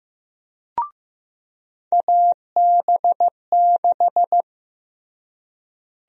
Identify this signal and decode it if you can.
Morse code, heard as a keyed tone, about 15 wpm, 700 Hz. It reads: AB6
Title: USA Call Sign Prefixes - Repeated 2x - 15wpm